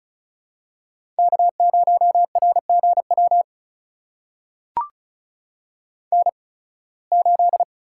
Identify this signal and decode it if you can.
Morse code, heard as a keyed tone, about 36 wpm, spaced as for 23 wpm, 690 Hz. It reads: K0RGW N8
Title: USA Call Signs - 5 Characters - 8x Character Spacing - 35wpm